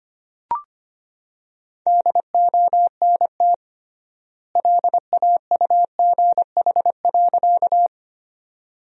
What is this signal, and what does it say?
Morse code, heard as a keyed tone, about 25 wpm, 700 Hz. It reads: DONT LAUGH.
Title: 3x Word-Spacing - Sentences from Top 500 Words - 25wpm